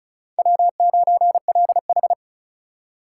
Morse code, heard as a keyed tone, about 36 wpm, 700 Hz. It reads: W9LH
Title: USA Call Signs - 4 Characters - 3x Character Spacing - 35wpm